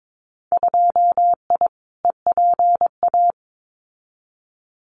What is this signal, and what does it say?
Morse code, heard as a keyed tone, about 22 wpm, 700 Hz. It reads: 2I EPA
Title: Winter Field Day Exchanges 22wpm